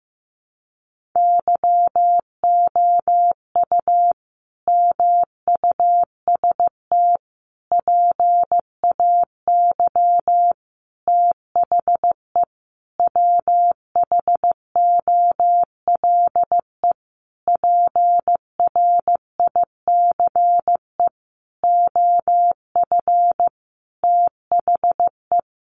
Morse code, heard as a keyed tone, about 15 wpm, 700 Hz. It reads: YOU MUST PAY THE WHOLE PRICE OF THE
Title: Sentences from Top 1400 Words 15wpm